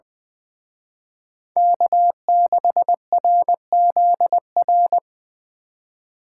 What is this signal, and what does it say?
Morse code, heard as a keyed tone, about 25 wpm, 700 Hz. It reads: K6RZR
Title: Speed Racing - USA 1x3 Call Signs 20wpm